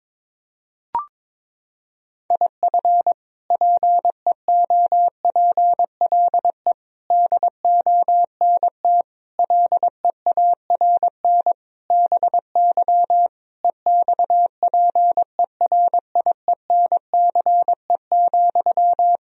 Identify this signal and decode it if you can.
Morse code, heard as a keyed tone, about 22 wpm, 700 Hz. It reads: IF PEOPLE DONT LEARN BY EXPERIENCE,
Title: Sentences from Top 1000 Words - Review Part 3 - 22wpm